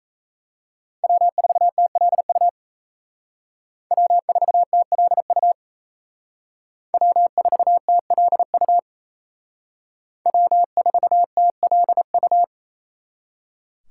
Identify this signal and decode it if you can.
Morse code, heard as a keyed tone, about 36 wpm, 700 Hz. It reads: W4TLU W4TLU W4TLU W4TLU